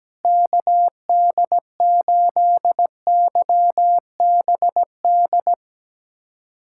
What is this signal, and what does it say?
Morse code, heard as a keyed tone, about 17 wpm, 700 Hz. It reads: KD8YBD
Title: USA Callsign DE Callsign - 6 Characters - 17wpm